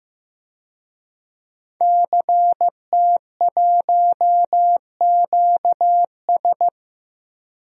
Morse code, heard as a keyed tone, about 15 wpm, 700 Hz. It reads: CT1QS
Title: DXCC-100 Call Signs - 5 Characters - 15wpm